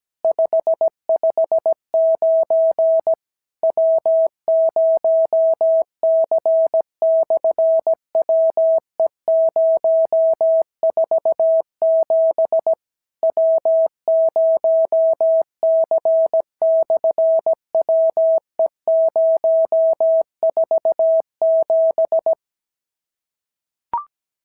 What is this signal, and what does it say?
Morse code, heard as a keyed tone, about 17 wpm, 650 Hz. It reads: 559 W0C/WE047 W0C/WE047
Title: SOTA QSO Practice 17wpm